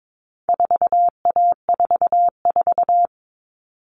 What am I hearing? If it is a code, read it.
Morse code, heard as a keyed tone, about 22 wpm, 700 Hz. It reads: 4A44
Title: All Cut Numbers - 4-Digit 22wpm